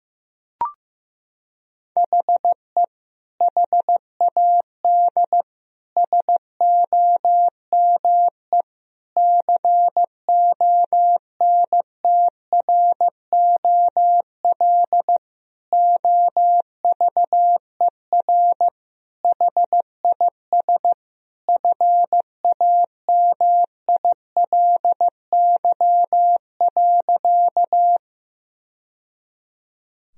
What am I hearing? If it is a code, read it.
Morse code, heard as a keyed tone, about 16 wpm, 700 Hz. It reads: HE HAD SOME CONTROL OVER HIS FAMILY.
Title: Sentences from Top 1000 Words - Review Part 3 - 15wpm